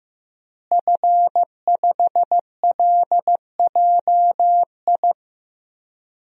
Morse code, heard as a keyed tone, about 15 wpm, 700 Hz. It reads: F5LJI
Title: DXCC-100 Call Signs - 5 Characters - Repeated 3x - 15wpm